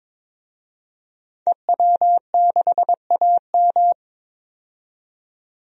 Morse code, heard as a keyed tone, about 22 wpm, 700 Hz. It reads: EW6AM
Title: DXCC-100 Call Signs - 5 Characters - 22wpm